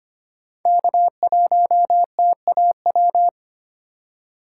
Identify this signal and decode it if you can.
Morse code, heard as a keyed tone, about 25 wpm, 700 Hz. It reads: K1TAW
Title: USA Call Signs - 5 Characters - 3x Character Spacing - 25wpm